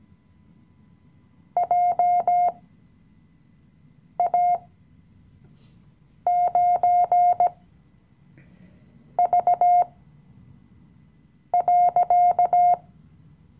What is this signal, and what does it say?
Morse code, heard as a keyed tone, about 17 wpm, spaced as for 5 wpm, 710 Hz. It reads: JA9V.
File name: CW-Online-Kurs / Anfänger / Lektion 018